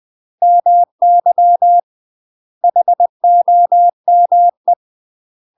Morse code, heard as a keyed tone, about 20 wpm, 700 Hz. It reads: MY HOME
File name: UK MASTER QSOs TRAINING / 20WPM / UK MASTER QSOs 0010